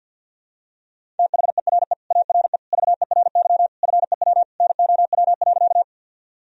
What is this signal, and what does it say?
Morse code, heard as a keyed tone, about 47 wpm, 700 Hz. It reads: THERE ARE VERY FEW NOW.
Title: Sentences from Top 1000 Words - Review Part 1 - 50wpm